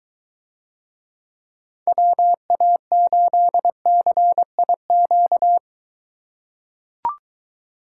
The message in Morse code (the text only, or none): WA8CIQ